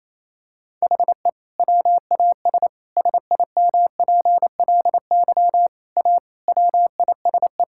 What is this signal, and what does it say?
Morse code, heard as a keyed tone, about 28 wpm, 700 Hz. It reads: HE WAS SIMPLY A WISE
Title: Sentences from Top 1600 Words 28wpm